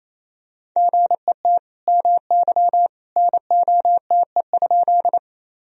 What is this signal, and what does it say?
Morse code, heard as a keyed tone, about 28 wpm, 700 Hz. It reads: GET MY NOTE?
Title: Sentences from Top 1000 Words - Review Part 2 - 28wpm